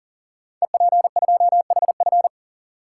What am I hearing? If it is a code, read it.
Morse code, heard as a keyed tone, about 40 wpm, 700 Hz. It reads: EP2HF
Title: Code Groups - 5 Letter:Numbers - 40wpm